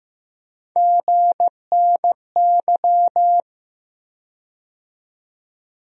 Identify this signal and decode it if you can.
Morse code, heard as a keyed tone, about 15 wpm, 700 Hz. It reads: GNY
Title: USA Call Sign Postfixes - 3 Letters 15wpm